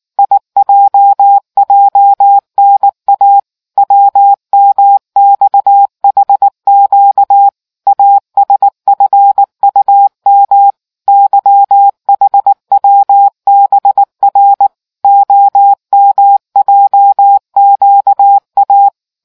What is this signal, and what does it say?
Morse code, heard as a keyed tone, about 19 wpm, 800 Hz. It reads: IJJNA WMXHQ ASFUM YHWBR OMJQA